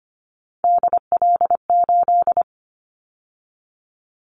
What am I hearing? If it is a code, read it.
Morse code, heard as a keyed tone, about 25 wpm, 700 Hz. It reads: DL8
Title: Callsign Prefixes for Top 25 DXCC - 25wpm